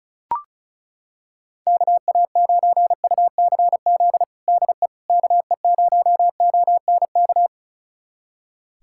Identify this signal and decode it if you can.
Morse code, heard as a keyed tone, about 36 wpm, 700 Hz. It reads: KA9UCZ DE KE0ONK
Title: USA Callsign DE Callsign - 6 Characters - 35wpm